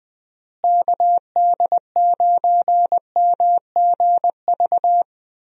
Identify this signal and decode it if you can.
Morse code, heard as a keyed tone, about 19 wpm, 700 Hz. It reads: KD9MGV
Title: USA Call Signs - 6 Characters - Repeated 2x - 20wpm